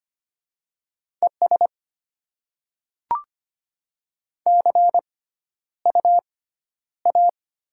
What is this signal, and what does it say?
Morse code, heard as a keyed tone, about 25 wpm, spaced as for 13 wpm, 700 Hz. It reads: ES CU A
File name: Top 20 QSO Elements - 6x Character Spacing - 25wpm